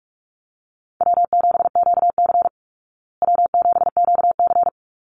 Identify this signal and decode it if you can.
Morse code, heard as a keyed tone, about 44 wpm, 700 Hz. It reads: W7QC W7QC
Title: Call Signs Repeated 2x - 45wpm